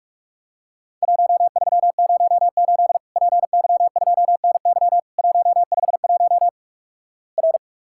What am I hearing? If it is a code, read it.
Morse code, heard as a keyed tone, about 46 wpm, 700 Hz. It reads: 1308 PY2NY 151 R
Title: Worked All Europe DX Contest - QTC Exchange - 45wpm